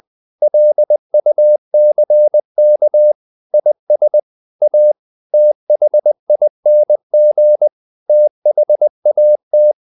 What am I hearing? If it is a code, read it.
Morse code, heard as a keyed tone, about 20 wpm, 600 Hz. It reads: LUCK IS A THING THAT
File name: the old man and the sea part 2 20wpm